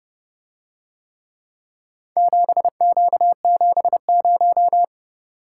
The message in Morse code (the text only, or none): 7Q70